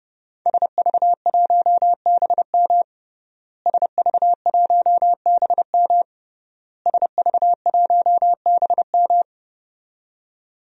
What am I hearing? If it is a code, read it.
Morse code, heard as a keyed tone, about 30 wpm, 700 Hz. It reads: SV1BM SV1BM SV1BM
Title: DXCC-100 Call Signs - 5 Characters - Repeated 3x - 30wpm